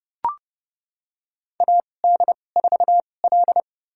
Morse code, heard as a keyed tone, about 30 wpm, spaced as for 22 wpm, 690 Hz. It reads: AD4L
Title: USA Call Signs - 4 Characters - 2x Character Spacing - 30wpm